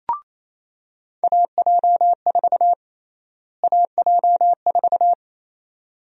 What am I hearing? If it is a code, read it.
Morse code, heard as a keyed tone, about 28 wpm, 700 Hz. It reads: AJ4 AJ4
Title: USA Call Sign Prefixes - Repeated 2x - 28wpm